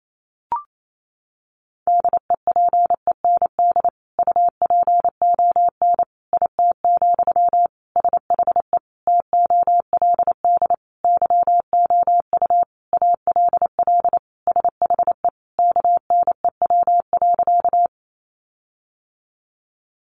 Morse code, heard as a keyed tone, about 28 wpm, 700 Hz. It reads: DEPEND UPON IT, SHE TOLD YOU ALL SHE KNEW.
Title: Sentences from Top 1300 Words 28wpm